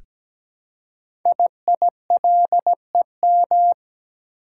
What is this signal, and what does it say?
Morse code, heard as a keyed tone, about 17 wpm, 700 Hz. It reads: IILEM
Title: Code Groups - 5 Letter:Numbers - 17wpm